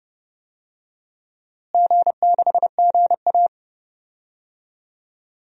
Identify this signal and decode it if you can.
Morse code, heard as a keyed tone, about 29 wpm, 700 Hz. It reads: G6GA
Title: DXCC-100 Call Signs - 4 Characters - Repeated 3x - 30wpm